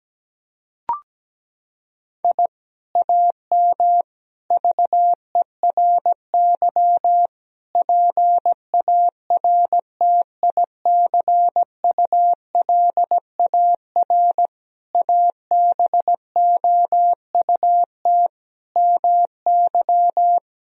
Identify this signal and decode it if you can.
Morse code, heard as a keyed tone, about 17 wpm, 700 Hz. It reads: I AM VERY PARTICULAR ABOUT MY
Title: Sentences from Top 1000 Words - Review Part 4 - 17wpm